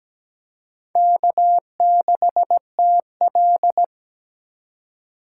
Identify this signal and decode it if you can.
Morse code, heard as a keyed tone, about 20 wpm, 700 Hz. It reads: K6TL
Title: Speed Racing - USA 1x2 Call Signs 17wpm